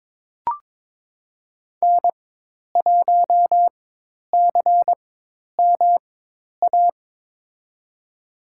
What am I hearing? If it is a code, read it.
Morse code, heard as a keyed tone, about 22 wpm, spaced as for 10 wpm, 700 Hz. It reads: N1CMA
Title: USA Call Signs - 4x Character Spacing - 22wpm